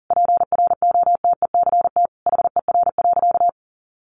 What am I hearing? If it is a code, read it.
Morse code, heard as a keyed tone, about 40 wpm, 700 Hz. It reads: PROTECT HER.
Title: Sentences from Top 900 Words 40wpm